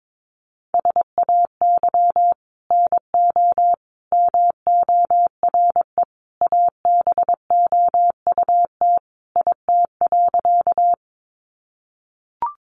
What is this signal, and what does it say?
Morse code, heard as a keyed tone, about 22 wpm, 700 Hz. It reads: SAY NO MORE ABOUT IT.
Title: Sentences from Top 200 Words 22wpm